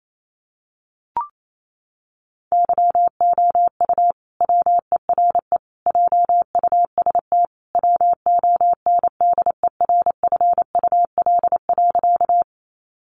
Morse code, heard as a keyed tone, about 28 wpm, 700 Hz. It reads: YOU WERE JUST WONDERFUL.
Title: Sentences from Top 1100 Words 28wpm